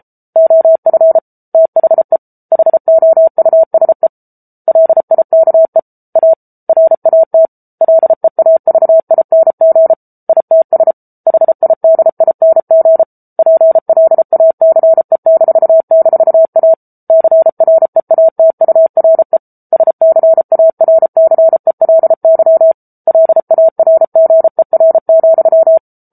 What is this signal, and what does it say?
Morse code, heard as a keyed tone, about 34 wpm, 650 Hz. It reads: OF THE HOUSE LIKE A RAT LEAVING ITS HIDING PLACE--A CREATURE SCARCELY LARGER,
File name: Book Two / 07-Chapter 7 - 33 WPM